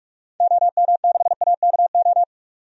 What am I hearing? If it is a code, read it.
Morse code, heard as a keyed tone, about 44 wpm, 700 Hz. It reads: OM6AKO
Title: DXCC-100 Call Signs - 6 Characters - 45wpm